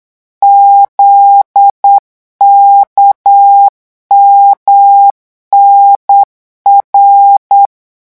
Morse code, heard as a keyed tone, about 8 wpm, 800 Hz. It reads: ZKMNR
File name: morse mp3 / revision8-vitesse 8 mots